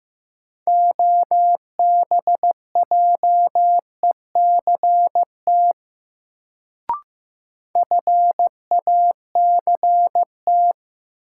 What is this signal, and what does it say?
Morse code, heard as a keyed tone, about 15 wpm, 700 Hz. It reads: OBJECT FACT
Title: Sentences from Top 500 Words - New Word Drill - 15wpm